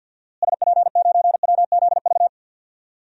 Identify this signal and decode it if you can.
Morse code, heard as a keyed tone, about 52 wpm, 700 Hz. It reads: SP9WZV
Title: DXCC-100 Call Signs - 6 Characters - Repeated 2x - 50wpm